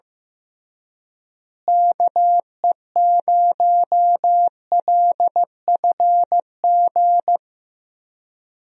Morse code, heard as a keyed tone, about 15 wpm, 700 Hz. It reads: KE0LFG
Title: USA Call Signs - 6 Characters - Repeated 2x - 15wpm